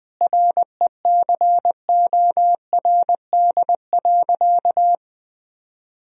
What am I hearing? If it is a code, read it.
Morse code, heard as a keyed tone, about 21 wpm, 700 Hz. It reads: RECORD.
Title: Sentences from Top 1000 Words - Review Part 3 - 20wpm